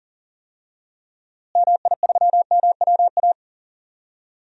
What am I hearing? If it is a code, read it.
Morse code, heard as a keyed tone, about 40 wpm, 700 Hz. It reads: MI3MWA
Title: DXCC-100 Call Signs - 6 Characters - Repeated 3x - 40wpm